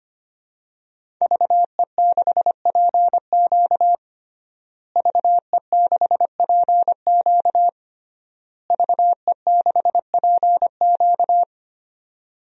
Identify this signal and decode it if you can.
Morse code, heard as a keyed tone, about 25 wpm, 700 Hz. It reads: VE6PQ VE6PQ VE6PQ
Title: DXCC-100 Call Signs - 5 Characters - Repeated 3x - 25wpm